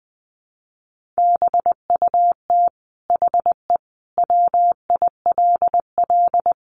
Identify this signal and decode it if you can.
Morse code, heard as a keyed tone, about 20 wpm, 700 Hz. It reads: BUT HE WILL
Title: Sentences from Top 1200 Words 20wpm